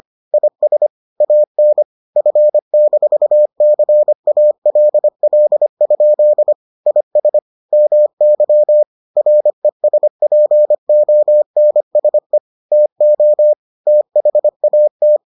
Morse code, heard as a keyed tone, about 25 wpm, 600 Hz. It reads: IS AN F-CALL? IS MY RESPONSE TO THAT